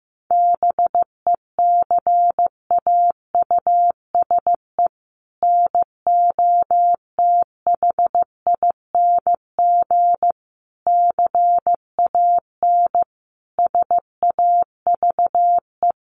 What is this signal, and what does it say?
Morse code, heard as a keyed tone, about 15 wpm, 700 Hz. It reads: BECAUSE NOTHING CAN SAVE
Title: Sentences from Top 1100 Words 15wpm